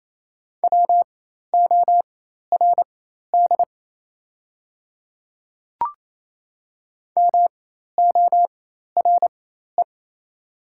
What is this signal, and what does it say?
Morse code, heard as a keyed tone, about 28 wpm, spaced as for 13 wpm, 690 Hz. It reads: WORD MORE
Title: 4x Character Spacing - Top 100 Words - 28wpm